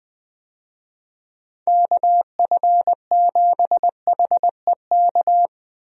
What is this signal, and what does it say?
Morse code, heard as a keyed tone, about 20 wpm, 700 Hz. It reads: KF7HEK
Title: USA Call Signs - 6 Characters - Repeated 3x - 20wpm